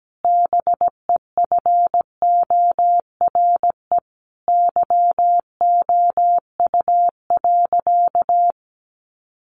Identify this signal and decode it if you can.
Morse code, heard as a keyed tone, about 17 wpm, 700 Hz. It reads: BEFORE YOU.
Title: Sentences from Top 1400 Words 17wpm